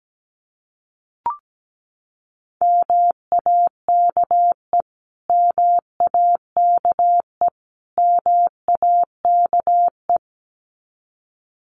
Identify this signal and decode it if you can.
Morse code, heard as a keyed tone, about 17 wpm, 700 Hz. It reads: MAKE MAKE MAKE
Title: Top 200 Words Repeated 17wpm